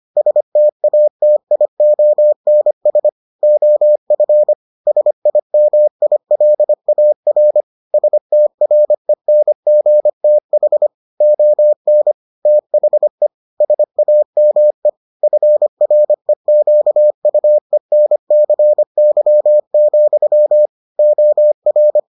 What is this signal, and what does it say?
Morse code, heard as a keyed tone, about 25 wpm, 600 Hz. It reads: STATIONS OF SIMILAR STRENGTH ON THE SAME FREQUENCY, OR